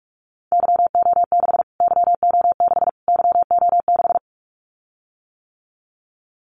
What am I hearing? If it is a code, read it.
Morse code, heard as a keyed tone, about 44 wpm, 700 Hz. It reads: YO6 YO6 YO6
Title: Callsign Prefixes for Top 25 DXCC - Repeated 3x - 45wpm